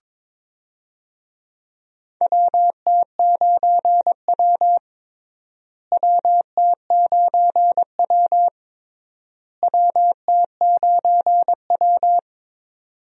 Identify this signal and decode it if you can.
Morse code, heard as a keyed tone, about 22 wpm, 700 Hz. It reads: WT9W WT9W WT9W
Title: USA Call Signs - 4 Characters - Repeated 3x - 22wpm